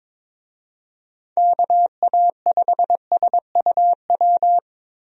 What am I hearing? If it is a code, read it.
Morse code, heard as a keyed tone, about 22 wpm, 700 Hz. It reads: KA5SUW
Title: USA Call Signs - 6 Characters - 22wpm